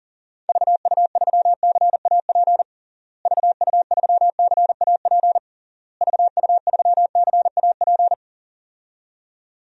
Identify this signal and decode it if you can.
Morse code, heard as a keyed tone, about 37 wpm, 700 Hz. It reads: VU3CAP VU3CAP VU3CAP
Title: DXCC-100 Call Signs - 6 Characters - Repeated 3x - 40wpm